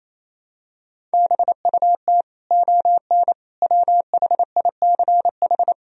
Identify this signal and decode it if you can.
Morse code, heard as a keyed tone, about 28 wpm, 700 Hz. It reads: BUT ON WHICH
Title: Sentences from Top 1000 Words - Review Part 1 - 28wpm